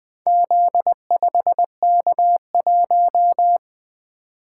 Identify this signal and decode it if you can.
Morse code, heard as a keyed tone, about 20 wpm, 700 Hz. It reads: Z5K1
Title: Four Letter-Number Combinations - 20wpm